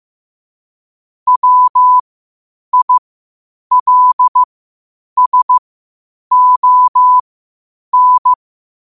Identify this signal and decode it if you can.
Morse code, heard as a keyed tone, about 15 wpm, spaced as for 8 wpm, 1000 Hz. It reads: WILSON
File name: qotdfiles / 8wpm